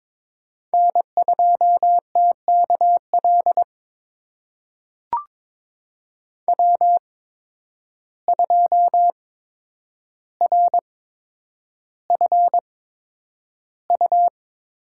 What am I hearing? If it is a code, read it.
Morse code, heard as a keyed tone, about 22 wpm, spaced as for 9 wpm, 700 Hz. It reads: N2TKL W2RFU
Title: USA Call Signs - 5 Characters - 8x Character Spacing - 22wpm